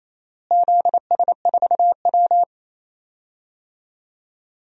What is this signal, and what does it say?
Morse code, heard as a keyed tone, about 28 wpm, 700 Hz. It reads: ZS4W
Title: DXCC-100 Call Signs - 4 Characters - Repeated 3x - 28wpm